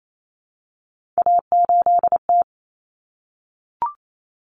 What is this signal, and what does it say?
Morse code, heard as a keyed tone, about 28 wpm, 700 Hz. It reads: A8T
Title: Common Cut Numbers - 3-Digit 28wpm